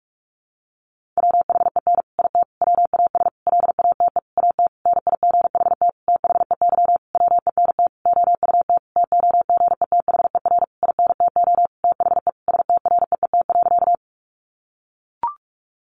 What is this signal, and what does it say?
Morse code, heard as a keyed tone, about 44 wpm, 700 Hz. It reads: WHEN IT WAS LATE AT NIGHT THEY WENT OUT TOGETHER INTO THE STREET.
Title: Sentences from Top 600 Words 45wpm